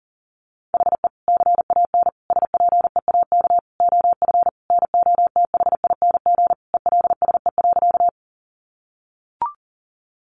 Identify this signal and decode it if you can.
Morse code, heard as a keyed tone, about 40 wpm, 700 Hz. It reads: HE CAN SPEAK OF NOTHING ELSE.
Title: Sentences from Top 1100 Words 40wpm